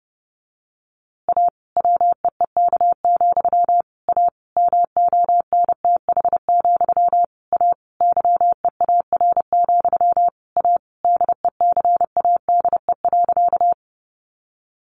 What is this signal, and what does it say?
Morse code, heard as a keyed tone, about 30 wpm, 700 Hz. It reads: A WEEK, A MONTH, A YEAR, A DECADE.
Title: A Sound Of Thunder 30wpm